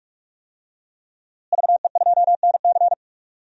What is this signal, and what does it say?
Morse code, heard as a keyed tone, about 42 wpm, 700 Hz. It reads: VE2NC
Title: DXCC-100 Call Signs - 5 Characters - Repeated 2x - 45wpm